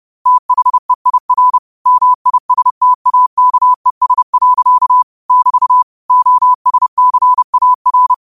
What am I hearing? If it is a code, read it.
Morse code, heard as a keyed tone, about 30 wpm, 1000 Hz. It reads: THEIR MISTAKES. <BT> OSCAR
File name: qotdfiles / 30wpm